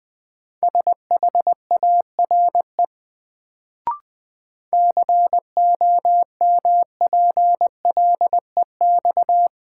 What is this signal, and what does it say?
Morse code, heard as a keyed tone, about 20 wpm, 700 Hz. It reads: SHARE COMPLEX
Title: Sentences from Top 900 Words - New Word Drill - 20wpm